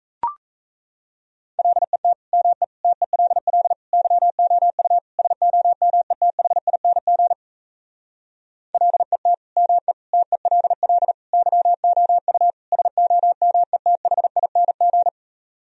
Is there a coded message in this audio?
Yes: Morse code from 1.6 to 15.1 s, heard as a keyed tone, about 37 wpm, 700 Hz. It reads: LET ME TELL YOU SOMETHING LET ME TELL YOU SOMETHING